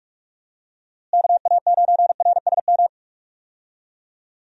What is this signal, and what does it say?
Morse code, heard as a keyed tone, about 44 wpm, 700 Hz. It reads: KA9RSM
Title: USA Call Signs - 6 Characters - Repeated 3x - 45wpm